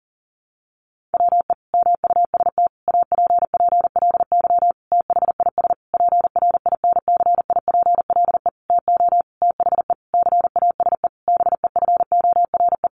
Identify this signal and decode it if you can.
Morse code, heard as a keyed tone, about 40 wpm, 700 Hz. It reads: WE MUST APPLY THIS PRINCIPLE TO THE CASE BEFORE
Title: Sentences from Top 1400 Words 40wpm